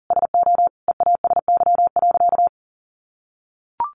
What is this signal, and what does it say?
Morse code, heard as a keyed tone, about 40 wpm, 700 Hz. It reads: SO EASY.
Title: Sentences from Top 700 Words 40wpm